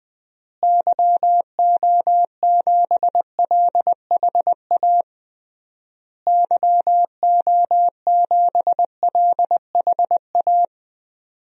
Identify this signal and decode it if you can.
Morse code, heard as a keyed tone, about 20 wpm, 700 Hz. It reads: YO7LHA YO7LHA
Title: DXCC-100 Call Signs - 6 Characters - Repeated 3x - 20wpm